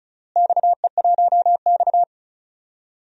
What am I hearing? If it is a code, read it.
Morse code, heard as a keyed tone, about 36 wpm, 700 Hz. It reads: XE1X
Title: DXCC-100 Call Signs - 4 Characters - Repeated 3x - 35wpm